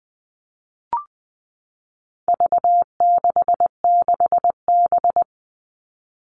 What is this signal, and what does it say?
Morse code, heard as a keyed tone, about 20 wpm, 700 Hz. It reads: V66B
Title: All Cut Numbers - 4-Digit 20wpm